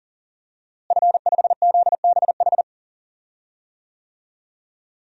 Morse code, heard as a keyed tone, about 40 wpm, 700 Hz. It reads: F5ZBH
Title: DXCC-100 Call Signs - 5 Characters - Repeated 2x - 40wpm